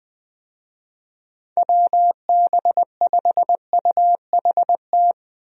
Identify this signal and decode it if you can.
Morse code, heard as a keyed tone, about 20 wpm, 700 Hz. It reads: WB5UHT
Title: USA Call Signs - 6 Characters - 20wpm